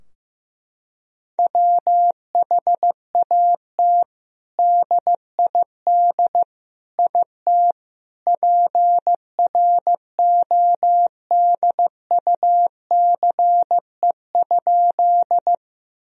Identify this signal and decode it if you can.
Morse code, heard as a keyed tone, about 15 wpm, 700 Hz. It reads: WHAT DID IT PRODUCE?
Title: Sentences from Top 1000 Words - Review Part 3 - 15wpm